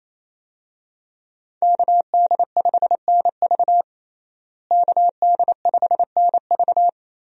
Morse code, heard as a keyed tone, about 27 wpm, 700 Hz. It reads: KD5NV KD5NV
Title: USA Call Signs - 5 Characters - Repeated 3x - 28wpm